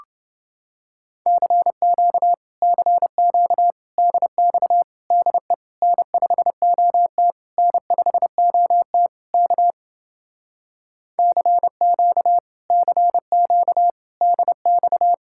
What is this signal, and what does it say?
Morse code, heard as a keyed tone, about 31 wpm, 700 Hz. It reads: CQ CQ DX DE N5OT N5OT K CQ CQ DX